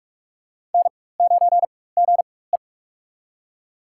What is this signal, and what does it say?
Morse code, heard as a keyed tone, about 43 wpm, spaced as for 21 wpm, 700 Hz. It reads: N9GE
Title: USA Call Signs - 4 Characters - 4x Character Spacing - 45wpm